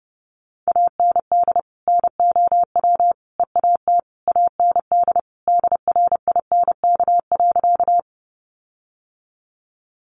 Morse code, heard as a keyed tone, about 30 wpm, 700 Hz. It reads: AND NOW EAT AND DRINK.
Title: Sentences from Top 1500 Words 30wpm